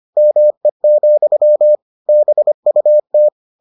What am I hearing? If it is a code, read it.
Morse code, heard as a keyed tone, about 25 wpm, 600 Hz. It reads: ME, BUT